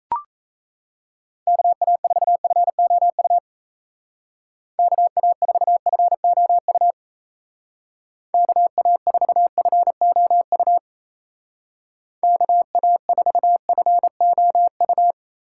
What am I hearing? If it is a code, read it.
Morse code, heard as a keyed tone, about 34 wpm, 700 Hz. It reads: KA4FOU KA4FOU KA4FOU KA4FOU
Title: Speed Racing - USA 2x3 Call Signs 28wpm